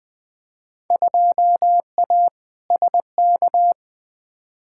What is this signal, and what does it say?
Morse code, heard as a keyed tone, about 21 wpm, 700 Hz. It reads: 2A SK
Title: Field Day Exchanges Encore - 20wpm